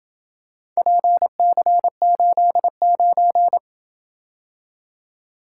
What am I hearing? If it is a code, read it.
Morse code, heard as a keyed tone, about 26 wpm, 700 Hz. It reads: PC89